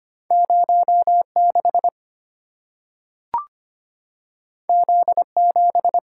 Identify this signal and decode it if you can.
Morse code, heard as a keyed tone, about 25 wpm, 700 Hz. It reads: 06 Z7
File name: Two Letter-Number Combinations - 25wpm